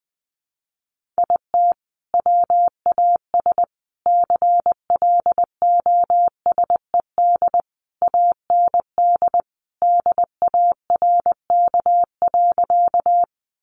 Morse code, heard as a keyed tone, about 19 wpm, 700 Hz. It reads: IT WAS CLOSED AND DARK.
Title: Sentences from Top 1400 Words 20wpm